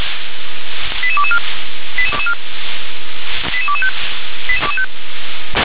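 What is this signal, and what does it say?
Five-tone selective call (ZVEI), heard as quick runs of five tones, tones of about 70 ms. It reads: 88224 88225
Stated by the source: ZVEI five-tone call 88221 received by SDR.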